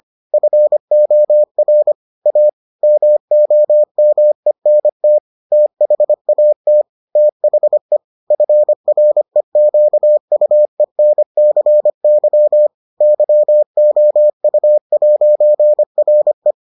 Morse code, heard as a keyed tone, about 25 wpm, 600 Hz. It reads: FOR A MOMENT THAT THE FREQUENCY YOU'RE